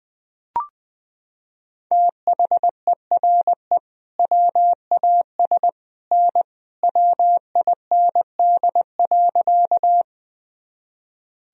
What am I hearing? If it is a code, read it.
Morse code, heard as a keyed tone, about 19 wpm, 700 Hz. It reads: THERE WAS N WIND.
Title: Sentences from Top 400 Words and Abbreviations 20wpm